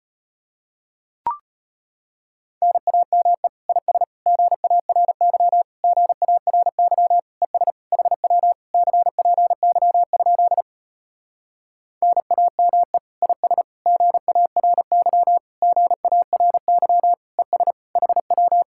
Morse code, heard as a keyed tone, about 33 wpm, 700 Hz. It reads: NAME IS GARY GARY ES HW CPY? NAME IS GARY GARY ES HW